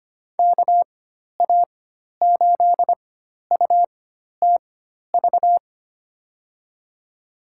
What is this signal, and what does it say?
Morse code, heard as a keyed tone, about 25 wpm, spaced as for 12 wpm, 700 Hz. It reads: KA8UTV
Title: USA Call Signs - 4x Character Spacing - 25wpm